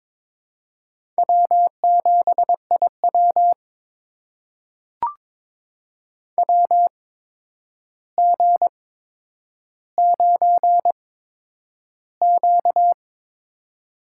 Morse code, heard as a keyed tone, about 22 wpm, spaced as for 9 wpm, 700 Hz. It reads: W7IW WG9Q
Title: USA Call Signs - 4 Characters - 8x Character Spacing - 22wpm